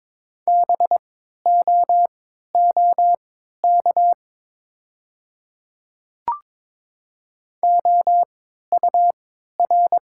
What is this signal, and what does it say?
Morse code, heard as a keyed tone, about 22 wpm, spaced as for 12 wpm, 700 Hz. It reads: BOOK OUR
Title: 3x Character Spacing - Top 300 Words - 22wpm